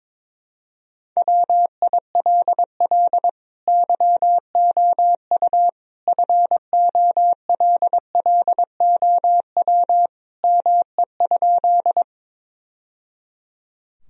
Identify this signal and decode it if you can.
Morse code, heard as a keyed tone, about 22 wpm, 700 Hz. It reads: WILL YOU FOLLOW ME?
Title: Sentences from Top 1000 Words - Review Part 1 - 22wpm